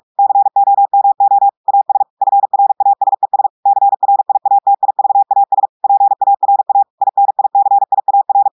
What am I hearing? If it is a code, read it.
Morse code, heard as a keyed tone, about 46 wpm, 800 Hz. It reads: COMO AS FRASES CRIATIVAS PARA INICIAR
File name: audios / portugais / portugais texte en clair 2 vitesse 45 mots minutes